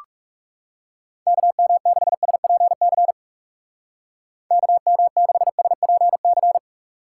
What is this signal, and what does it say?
Morse code, heard as a keyed tone, about 37 wpm, 700 Hz. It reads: KM6SPC KM6SPC
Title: Speed Racing - USA Call Signs 30wpm